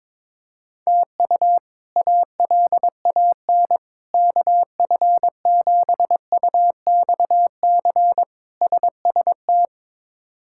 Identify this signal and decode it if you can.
Morse code, heard as a keyed tone, about 22 wpm, 700 Hz. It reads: TU ALAN KF7UXC SST
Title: K1USN SST QSO Practice - 22wpm